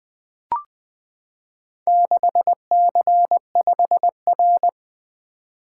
Morse code, heard as a keyed tone, about 20 wpm, 700 Hz. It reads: 6C5R
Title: Four Letter-Number Combinations - 20wpm